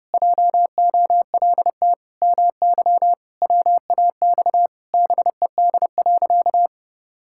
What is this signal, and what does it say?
Morse code, heard as a keyed tone, about 30 wpm, 700 Hz. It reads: JOLT MY WAX BED.
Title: Pangram Challenge v2 - 30wpm